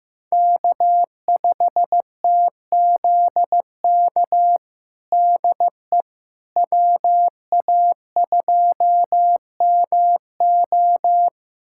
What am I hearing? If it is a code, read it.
Morse code, heard as a keyed tone, about 16 wpm, 700 Hz. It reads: K5TZK DE WA2MO
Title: USA Callsign DE Callsign - 5 Characters - 15wpm